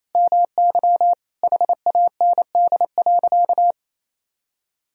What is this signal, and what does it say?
Morse code, heard as a keyed tone, about 28 wpm, 700 Hz. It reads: MY HAND.
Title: Sentences from Top 1000 Words - Review Part 1 - 28wpm